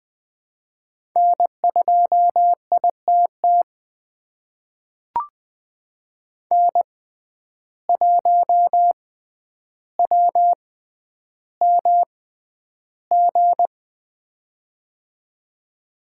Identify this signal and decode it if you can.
Morse code, heard as a keyed tone, about 21 wpm, spaced as for 10 wpm, 700 Hz. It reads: N2ITT N1WMG
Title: USA Call Signs - 5 Characters - 6x Character Spacing - 20wpm